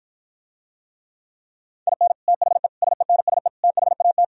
Morse code, heard as a keyed tone, about 52 wpm, 700 Hz. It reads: IN THE SENSE THAT